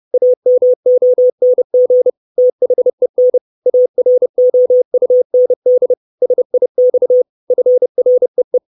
Morse code, heard as a keyed tone, about 30 wpm, 500 Hz. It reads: AMONG THEN AROUND SIX FREE
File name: 632 Words 30WPM